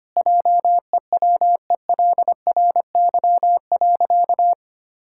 Morse code, heard as a keyed tone, about 25 wpm, 700 Hz. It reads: JEWELRY.
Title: Pangram Challenge v2 - 25wpm